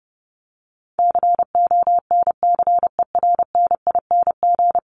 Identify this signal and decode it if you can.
Morse code, heard as a keyed tone, about 30 wpm, 700 Hz. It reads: CONCERNING
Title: Top 1751-2000 Words 30wpm